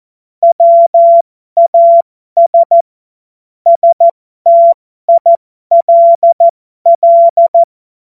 Morse code, heard as a keyed tone, about 14 wpm, spaced as for 12 wpm, 680 Hz. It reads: WAS STILL
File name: Morsekurs / Witze